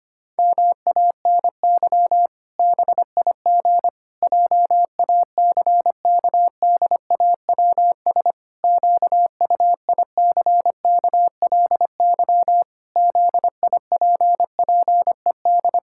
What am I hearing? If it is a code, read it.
Morse code, heard as a keyed tone, about 25 wpm, 700 Hz. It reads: MANY BIG JACKDAWS QUICKLY ZIPPED
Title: Pangram Challenge v2 - 25wpm